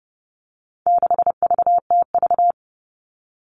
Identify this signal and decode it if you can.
Morse code, heard as a keyed tone, about 30 wpm, 700 Hz. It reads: 6VTV
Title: All Cut Numbers - 4-Digit 30wpm